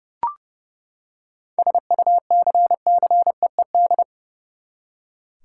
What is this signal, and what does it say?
Morse code, heard as a keyed tone, about 30 wpm, 700 Hz. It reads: SUCCEED
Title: Seven Letter Words - 30wpm